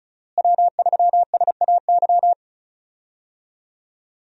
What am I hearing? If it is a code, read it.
Morse code, heard as a keyed tone, about 34 wpm, 700 Hz. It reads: W3SAY
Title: USA Call Signs - 5 Characters - Repeated 2x - 35wpm